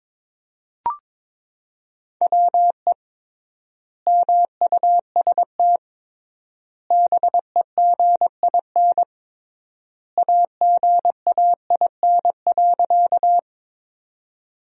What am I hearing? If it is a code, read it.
Morse code, heard as a keyed tone, about 22 wpm, 700 Hz. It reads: WE MUST BEGIN AGAIN.
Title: 3x Word-Spacing - Sentences from Top 300 Words - 22wpm